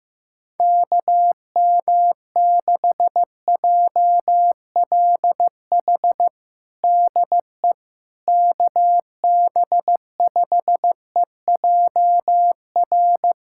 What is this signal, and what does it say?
Morse code, heard as a keyed tone, about 15 wpm, 700 Hz. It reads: KM6JLH DE KB5EJR
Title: USA Callsign DE Callsign - 6 Characters - 15wpm